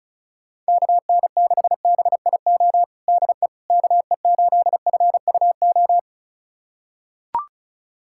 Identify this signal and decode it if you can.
Morse code, heard as a keyed tone, about 34 wpm, 700 Hz. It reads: KN6BIO DE KE8FUO
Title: USA Callsign DE Callsign - 6 Characters - 35wpm